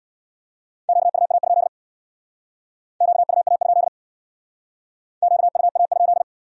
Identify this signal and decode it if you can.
Morse code, heard as a keyed tone, about 62 wpm, 700 Hz. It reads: QUA? QUA? QUA?
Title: Speed-Racing Q-Codes - 50wpm